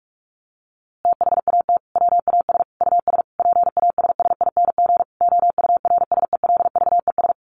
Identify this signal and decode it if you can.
Morse code, heard as a keyed tone, about 44 wpm, 700 Hz. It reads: THAT WAS US PASSING OURSELVES
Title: A Sound Of Thunder 45wpm